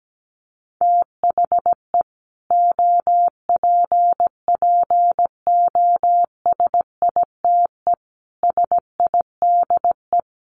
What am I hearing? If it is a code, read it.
Morse code, heard as a keyed tone, about 17 wpm, 700 Hz. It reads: THE OPPOSITE SIDE
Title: Sets of 5 Words Encore 17wpm